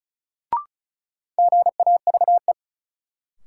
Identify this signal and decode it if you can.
Morse code, heard as a keyed tone, about 36 wpm, 700 Hz. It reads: GAVE
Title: Sentences from Top 500 Words - New Word Drill - 35wpm